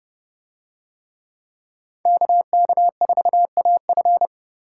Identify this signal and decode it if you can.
Morse code, heard as a keyed tone, about 30 wpm, 700 Hz. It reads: KK4AF